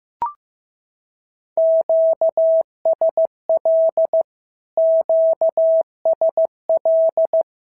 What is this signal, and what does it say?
Morse code, heard as a keyed tone, about 15 wpm, 650 Hz. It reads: QSL QSL
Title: SOTA QSO Practice 15wpm